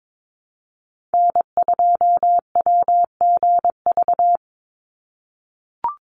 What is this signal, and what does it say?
Morse code, heard as a keyed tone, about 22 wpm, 700 Hz. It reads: N2WGV
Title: USA Call Signs 22wpm